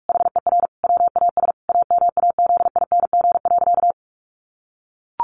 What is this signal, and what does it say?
Morse code, heard as a keyed tone, about 44 wpm, 700 Hz. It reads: HER WAS AMAZING.
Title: Sentences from Top 1500 Words 45wpm